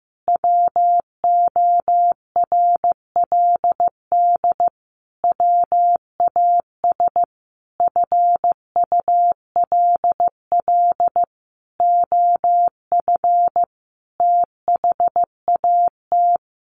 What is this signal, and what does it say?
Morse code, heard as a keyed tone, about 15 wpm, 700 Hz. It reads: WORLD WAS FULL OF THAT
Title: Sentences from Top 1300 Words 15wpm